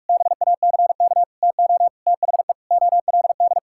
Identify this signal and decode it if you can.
Morse code, heard as a keyed tone, about 41 wpm, 700 Hz. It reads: BACK TO THE OLD
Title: Sentences from Top 1000 Words - Review Part 5 - 45wpm